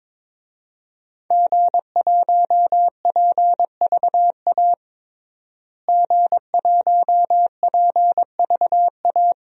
Morse code, heard as a keyed tone, about 22 wpm, 700 Hz. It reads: G1PVA G1PVA
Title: DXCC-100 Call Signs - 5 Characters - Repeated 2x - 22wpm